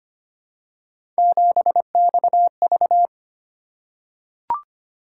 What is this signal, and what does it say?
Morse code, heard as a keyed tone, about 25 wpm, 700 Hz. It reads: 7XV
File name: Three Letter-Number Combinations - 25wpm